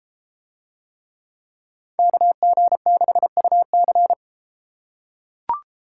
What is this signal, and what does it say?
Morse code, heard as a keyed tone, about 32 wpm, 700 Hz. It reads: KG6UC